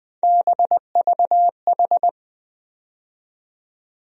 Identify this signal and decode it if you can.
Morse code, heard as a keyed tone, about 20 wpm, 700 Hz. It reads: BVH
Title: USA Call Sign Postfixes - 3 Letters 20wpm